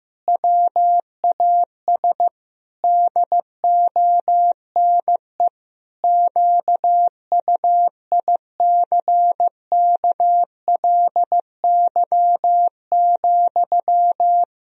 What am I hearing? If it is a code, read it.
Morse code, heard as a keyed tone, about 15 wpm, 700 Hz. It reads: WAS DONE QUICKLY,
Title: Sentences from Top 1000 Words - Review Part 2 - 15wpm